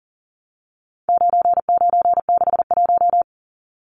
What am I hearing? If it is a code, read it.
Morse code, heard as a keyed tone, about 40 wpm, 700 Hz. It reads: 9961
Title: Numbers - 4-Digit 40wpm